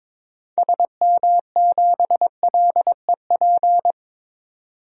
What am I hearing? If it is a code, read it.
Morse code, heard as a keyed tone, about 22 wpm, 700 Hz. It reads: SM7LEP
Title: DXCC-100 Call Signs - 6 Characters - 22wpm